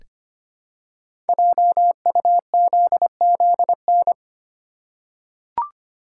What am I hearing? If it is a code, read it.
Morse code, heard as a keyed tone, about 25 wpm, 700 Hz. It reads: JUZZN
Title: Code Groups - 5 Letters - 25wpm